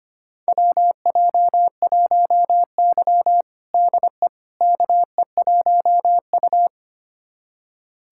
Morse code, heard as a keyed tone, about 25 wpm, 700 Hz. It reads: WJ1Y DE KE1U
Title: USA Callsign DE Callsign - 4 Characters - 25wpm